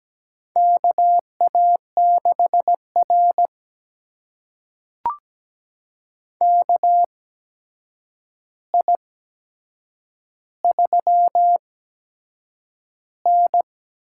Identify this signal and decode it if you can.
Morse code, heard as a keyed tone, about 17 wpm, spaced as for 7 wpm, 700 Hz. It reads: KA6R KI3N
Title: USA Call Signs - 4 Characters - 8x Character Spacing - 17wpm